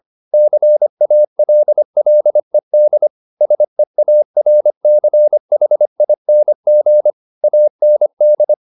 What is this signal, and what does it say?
Morse code, heard as a keyed tone, about 25 wpm, 600 Hz. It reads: CALLED SEARCHING AND